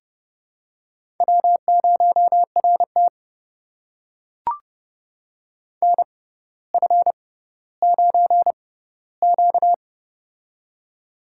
Morse code, heard as a keyed tone, about 31 wpm, spaced as for 16 wpm, 700 Hz. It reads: W0RT NF9Q
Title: USA Call Signs - 4 Characters - 6x Character Spacing - 30wpm